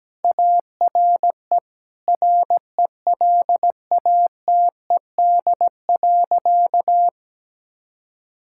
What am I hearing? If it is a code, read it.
Morse code, heard as a keyed tone, about 17 wpm, 700 Hz. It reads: ARE RELATED.
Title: Sentences from Top 1000 Words - Review Part 5 - 17wpm